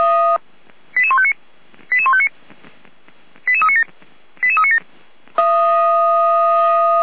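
Five-tone selective call (ZVEI), heard as quick runs of five tones, tones of about 70 ms. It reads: 88159 88159 88397 88397
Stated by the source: ZVEI five-tone call 88149 received by SDR.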